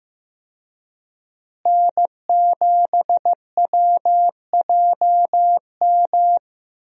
Morse code, heard as a keyed tone, about 16 wpm, 700 Hz. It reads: N7WJM
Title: USA Call Signs - 5 Characters - Repeated 2x - 15wpm